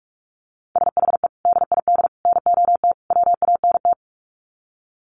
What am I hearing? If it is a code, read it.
Morse code, heard as a keyed tone, about 44 wpm, 700 Hz. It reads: SHE DID NOT WANT
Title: Sets of 4 Words 45wpm